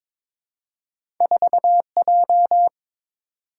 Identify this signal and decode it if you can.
Morse code, heard as a keyed tone, about 22 wpm, 700 Hz. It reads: 4J
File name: practice / 22wpm / Two Letter-Number Combinations - 22wpm